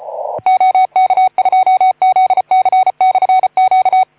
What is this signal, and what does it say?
Morse code, heard as a keyed tone, about 33 wpm, 750 Hz. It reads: OK2ZC/Q
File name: Archiv / CUC181 / OK2ZC CUC 181